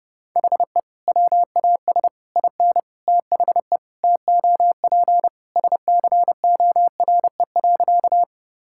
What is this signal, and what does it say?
Morse code, heard as a keyed tone, about 30 wpm, 700 Hz. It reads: HE WAS IN THE TOP SCORE.
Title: Sentences from Top 1600 Words 30wpm